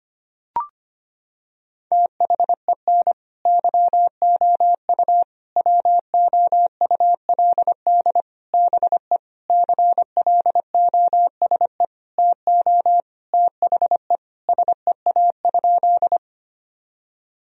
Morse code, heard as a keyed tone, about 25 wpm, 700 Hz. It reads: THEN YOU WOULD BE CLOSE TO THE SEA?
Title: Sentences from Top 1000 Words - Review Part 1 - 25wpm